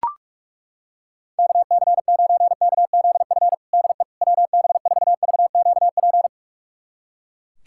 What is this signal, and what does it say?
Morse code, heard as a keyed tone, about 42 wpm, 700 Hz. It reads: KC9KZF DE WB4VQP
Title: USA Callsign DE Callsign - 6 Characters - 45wpm